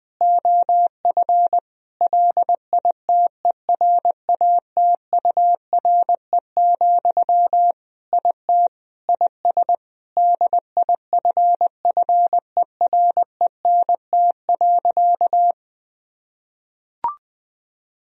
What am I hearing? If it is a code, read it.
Morse code, heard as a keyed tone, about 20 wpm, 700 Hz. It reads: OF LITERATURE, IT IS DIFFERENT.
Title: Sentences from Top 1000 Words - Review Part 3 - 20wpm